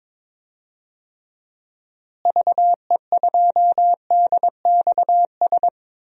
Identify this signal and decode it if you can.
Morse code, heard as a keyed tone, about 22 wpm, 700 Hz. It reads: VE2DXS
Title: DXCC-100 Call Signs - 6 Characters - Repeated 3x - 22wpm